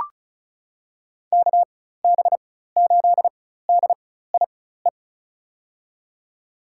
Morse code, heard as a keyed tone, about 36 wpm, spaced as for 16 wpm, 700 Hz. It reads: KB8DIE
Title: USA Call Signs - 6 Characters - 4x Character Spacing - 35wpm